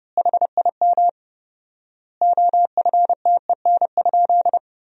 Morse code, heard as a keyed tone, about 32 wpm, 700 Hz. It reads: HIM OFTEN?
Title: 4x Word-Spacing - Sentences from Top 400 Words - 30wpm